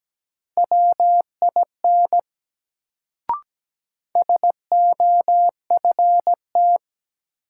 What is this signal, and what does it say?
Morse code, heard as a keyed tone, about 17 wpm, 700 Hz. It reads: WIN SOFT
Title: Sentences from Top 1500 Words - New Word Drill - 17wpm